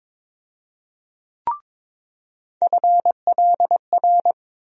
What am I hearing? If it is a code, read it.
Morse code, heard as a keyed tone, about 22 wpm, 700 Hz. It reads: FLR
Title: USA Call Sign Postfixes - 3 Letters 22wpm